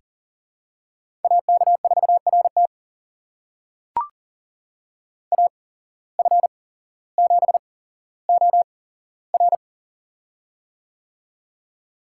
Morse code, heard as a keyed tone, about 40 wpm, spaced as for 17 wpm, 700 Hz. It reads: AK4RT AF7OR
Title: USA Call Signs - 8x Character Spacing - 40wpm